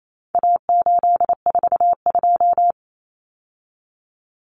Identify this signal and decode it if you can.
Morse code, heard as a keyed tone, about 28 wpm, 700 Hz. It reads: A842
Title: Common Cut Numbers - 4-Digit 28wpm